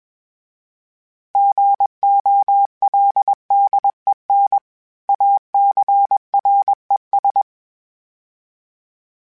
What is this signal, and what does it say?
Morse code, heard as a keyed tone, about 20 wpm, 800 Hz. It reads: GOLDEN ACRES
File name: fromSMs / FromSMs mp3s / Cities@20